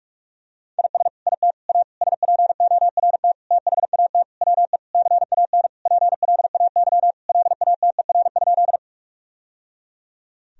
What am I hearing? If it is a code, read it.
Morse code, heard as a keyed tone, about 41 wpm, 700 Hz. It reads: IS IT A SPORT THAT WE CAN PLAY LATER?
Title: Sentences from Top 1000 Words - Review Part 4 - 45wpm